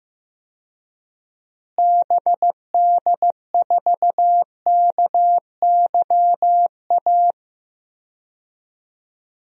Morse code, heard as a keyed tone, about 15 wpm, 700 Hz. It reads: BD4KYA
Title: DXCC-100 Call Signs - 6 Characters - Repeated 2x - 15wpm